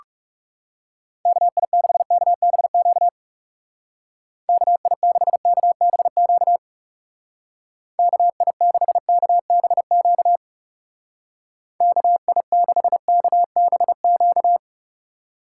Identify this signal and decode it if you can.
Morse code, heard as a keyed tone, about 35 wpm, 700 Hz. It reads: KI6KBQ KI6KBQ KI6KBQ KI6KBQ
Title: Speed Racing - USA 2x3 Call Signs 30wpm